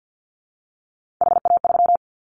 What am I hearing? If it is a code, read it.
Morse code, heard as a keyed tone, about 50 wpm, 700 Hz. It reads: HA3
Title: Callsign Prefixes for Top 50 DXCC - 50wpm